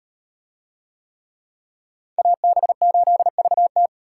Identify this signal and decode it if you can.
Morse code, heard as a keyed tone, about 38 wpm, 700 Hz. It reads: AB8VT